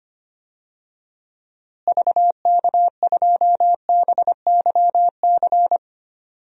Morse code, heard as a keyed tone, about 25 wpm, 700 Hz. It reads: VK2BYC
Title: DXCC-100 Call Signs - 6 Characters - Repeated 3x - 25wpm